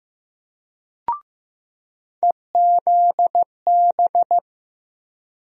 Morse code, heard as a keyed tone, about 15 wpm, 700 Hz. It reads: EZB
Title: USA Call Sign Postfixes - 3 Letters 15wpm